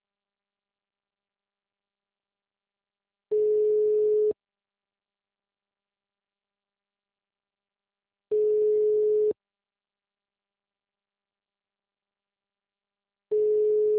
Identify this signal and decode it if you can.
Ringback tone (Europe, 425 Hz).
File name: out- -2002- - - .5599